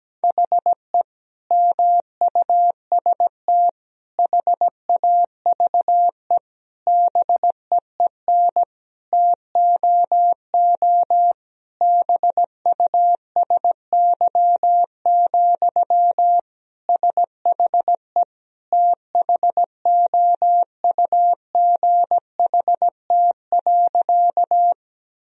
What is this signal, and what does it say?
Morse code, heard as a keyed tone, about 17 wpm, 700 Hz. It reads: HE MUST HAVE BEEN TOO BUSY, SHE THOUGHT.
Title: Sentences from Top 1000 Words - Review Part 3 - 17wpm